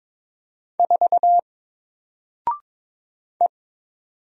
Morse code, heard as a keyed tone, about 22 wpm, 700 Hz. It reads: 4 E
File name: Single-Letter-Number-22wpm